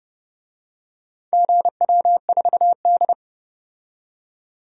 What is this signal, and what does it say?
Morse code, heard as a keyed tone, about 30 wpm, 700 Hz. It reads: GW4D
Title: DXCC-100 Call Signs - 4 Characters - 30wpm